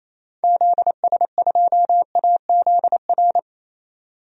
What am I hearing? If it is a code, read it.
Morse code, heard as a keyed tone, about 27 wpm, 700 Hz. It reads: ZS2AZR
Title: DXCC-100 Call Signs - 6 Characters - 28wpm